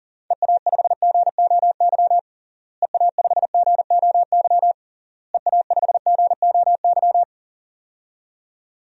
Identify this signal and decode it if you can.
Morse code, heard as a keyed tone, about 40 wpm, 700 Hz. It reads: EA5GOY EA5GOY EA5GOY
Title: DXCC-100 Call Signs - 6 Characters - Repeated 3x - 40wpm